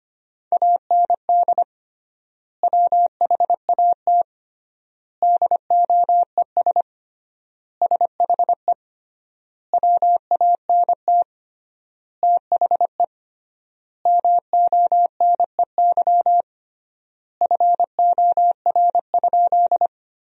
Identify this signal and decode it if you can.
Morse code, heard as a keyed tone, about 25 wpm, 700 Hz. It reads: AND WHAT DOES SHE WANT THE MONEY FOR?
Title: 3x Word-Spacing - Sentences from Top 400 Words - 25wpm